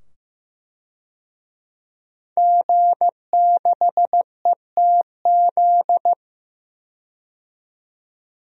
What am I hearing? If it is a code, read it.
Morse code, heard as a keyed tone, about 16 wpm, 700 Hz. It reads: G6ETZ
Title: DXCC-100 Call Signs - 5 Characters - 15wpm